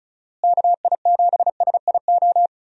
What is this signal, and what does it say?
Morse code, heard as a keyed tone, about 34 wpm, 700 Hz. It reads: KI7SIO
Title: USA Call Signs - 6 Characters - Repeated 2x - 35wpm